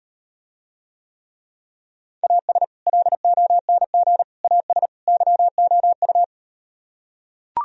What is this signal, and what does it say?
Morse code, heard as a keyed tone, about 38 wpm, 700 Hz. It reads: AS LONG AS YOU